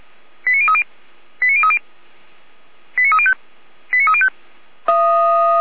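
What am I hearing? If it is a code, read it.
Five-tone selective call (ZVEI), heard as quick runs of five tones, tones of about 70 ms. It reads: 89030 89030 89305 89305
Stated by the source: ZVEI five-tone selective call received by SDR for the Feuerwehr Orlamünde fire service.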